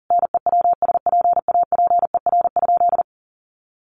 Morse code, heard as a keyed tone, about 40 wpm, 700 Hz. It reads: NEWSPAPER?
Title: Sentences from Top 1100 Words 40wpm